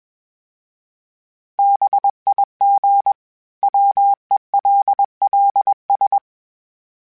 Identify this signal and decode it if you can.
Morse code, heard as a keyed tone, about 21 wpm, 800 Hz. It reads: BIG WELLS
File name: fromSMs / FromSMs mp3s / Cities@20